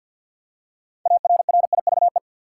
Morse code, heard as a keyed tone, about 49 wpm, 700 Hz. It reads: ARRIVE
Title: Six Letter Words - 50wpm